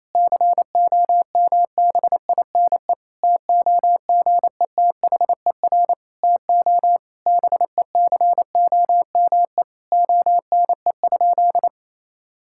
Morse code, heard as a keyed tone, about 28 wpm, 700 Hz. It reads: COMBINE TOGETHER TO BECOME ONE?
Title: Sentences from Top 1600 Words 28wpm